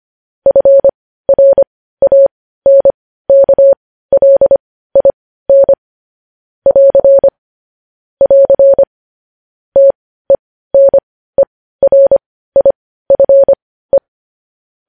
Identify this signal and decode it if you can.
Morse code, heard as a keyed tone, about 25 wpm, spaced as for 15 wpm, 550 Hz. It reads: FRANKLIN <AR> <AR> TENERIFE
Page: News Headlines in Morse Code at 15 WPM